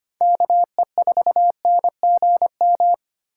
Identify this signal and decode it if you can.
Morse code, heard as a keyed tone, about 25 wpm, 700 Hz. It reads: KE4NGM
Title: USA Call Signs - 6 Characters - 8x Character Spacing - 25wpm